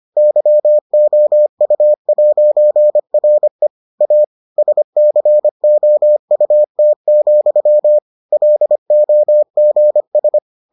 Morse code, heard as a keyed tone, about 25 wpm, 600 Hz. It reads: YOU'RE A SCOUT, LOGS